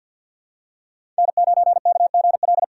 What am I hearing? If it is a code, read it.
Morse code, heard as a keyed tone, about 52 wpm, 700 Hz. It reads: N9KGL
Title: USA Call Signs - 5 Characters - 6x Character Spacing - 50wpm